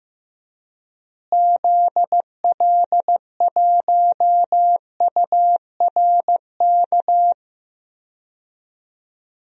Morse code, heard as a keyed tone, about 15 wpm, 700 Hz. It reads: ZL1URK
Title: DXCC-100 Call Signs - 6 Characters - Repeated 3x - 15wpm